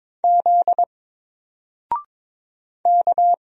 Morse code, heard as a keyed tone, about 22 wpm, 700 Hz. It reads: Z K
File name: Single-Letter-Number-22wpm